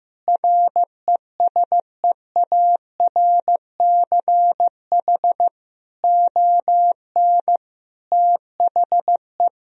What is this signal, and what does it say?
Morse code, heard as a keyed tone, about 15 wpm, 700 Hz. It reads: RESEARCH ON THE
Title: Sentences from Top 1000 Words - Review Part 4 - 15wpm